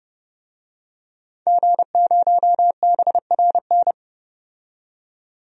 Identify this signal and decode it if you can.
Morse code, heard as a keyed tone, about 30 wpm, 700 Hz. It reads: G0BRN
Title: DXCC-100 Call Signs - 5 Characters - 30wpm